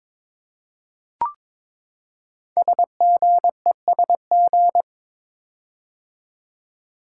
Morse code, heard as a keyed tone, about 22 wpm, 700 Hz. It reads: SGESG
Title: Code Groups - 5 Letters - 22wpm